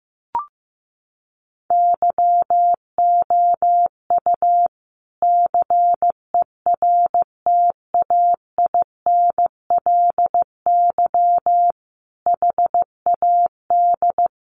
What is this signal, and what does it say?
Morse code, heard as a keyed tone, about 15 wpm, 700 Hz. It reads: YOU CERTAINLY HAD
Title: Sentences from Top 1200 Words 15wpm